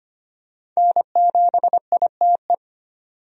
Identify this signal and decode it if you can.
Morse code, heard as a keyed tone, about 25 wpm, 700 Hz. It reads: N7ITE
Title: USA Call Signs - 5 Characters - 25wpm